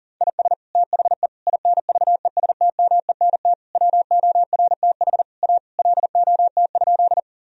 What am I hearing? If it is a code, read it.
Morse code, heard as a keyed tone, about 40 wpm, 700 Hz. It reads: IS THE INVESTMENT WORTH A LOT?
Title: Sentences from Top 1000 Words - Review Part 4 - 40wpm